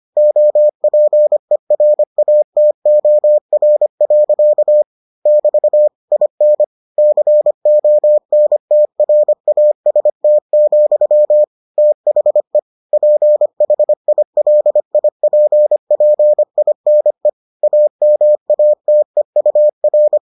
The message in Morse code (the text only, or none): OPERATOR. <BT> IN CONTRAST, THE PHILIPPINE AMATEUR